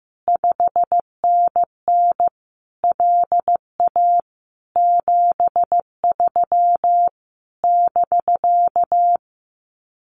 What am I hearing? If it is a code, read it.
Morse code, heard as a keyed tone, about 15 wpm, 700 Hz. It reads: 5NN LA 73 <BK>
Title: Common QSO Phrases Encore 15wpm